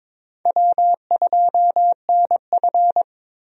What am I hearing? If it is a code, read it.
Morse code, heard as a keyed tone, about 22 wpm, 700 Hz. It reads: W2NF
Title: USA Call Signs - 4 Characters - Repeated 3x - 22wpm